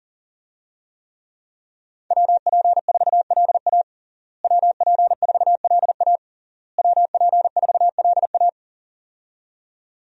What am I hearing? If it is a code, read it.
Morse code, heard as a keyed tone, about 40 wpm, 690 Hz. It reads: WP4LA WP4LA WP4LA
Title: USA Call Signs - 5 Characters - Repeated 3x - 40wpm